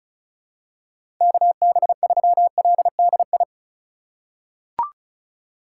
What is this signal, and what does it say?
Morse code, heard as a keyed tone, about 36 wpm, 700 Hz. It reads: KB3LDI
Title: USA Call Signs - 6 Characters - 2x Character Spacing - 35wpm